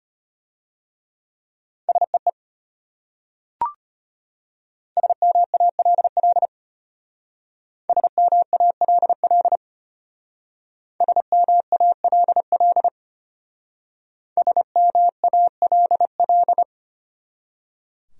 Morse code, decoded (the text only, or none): SEE SMALL SMALL SMALL SMALL